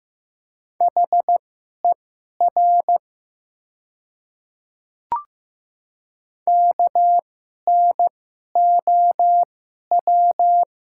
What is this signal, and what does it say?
Morse code, heard as a keyed tone, about 15 wpm, spaced as for 11 wpm, 700 Hz. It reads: HER KNOW
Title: 2x Character Spacing - Top 100 Words - 15wpm